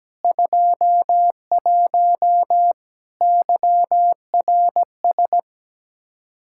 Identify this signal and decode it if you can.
Morse code, heard as a keyed tone, about 17 wpm, 700 Hz. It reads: 21 YRS
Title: Practice QSO - Exchange 3 - More Info - 17wpm